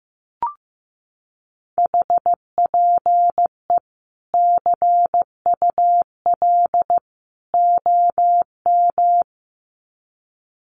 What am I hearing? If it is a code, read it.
Morse code, heard as a keyed tone, about 15 wpm, 700 Hz. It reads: HPE CUL OM
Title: Common QSO Phrases 15wpm